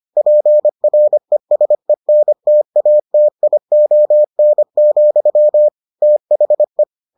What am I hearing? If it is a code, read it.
Morse code, heard as a keyed tone, about 25 wpm, 600 Hz. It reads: PRESENTATION, THE